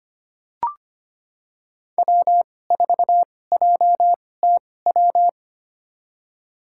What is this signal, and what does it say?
Morse code, heard as a keyed tone, about 25 wpm, spaced as for 18 wpm, 700 Hz. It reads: W4JTW
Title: USA Call Signs - 2x Character Spacing - 25wpm